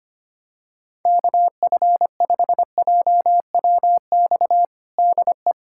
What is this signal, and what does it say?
Morse code, heard as a keyed tone, about 25 wpm, 700 Hz. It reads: KF5JWX DE
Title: USA Callsign DE Callsign 25wpm